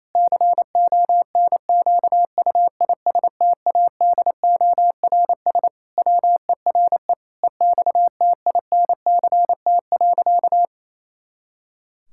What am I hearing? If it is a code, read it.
Morse code, heard as a keyed tone, about 28 wpm, 700 Hz. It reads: CONQUISTADORS WERE EXTINCT.
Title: Pangram Challenge v2 - 28wpm